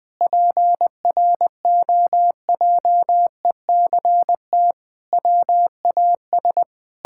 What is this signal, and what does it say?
Morse code, heard as a keyed tone, about 20 wpm, 700 Hz. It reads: PROJECT WAS
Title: Sentences with 9-Word Phrases - 20wpm